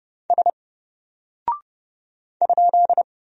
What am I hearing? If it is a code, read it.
Morse code, heard as a keyed tone, about 30 wpm, 700 Hz. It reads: S ?
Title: Single Character (Letter:Number:Punctuation:Prosign) Alt 30wpm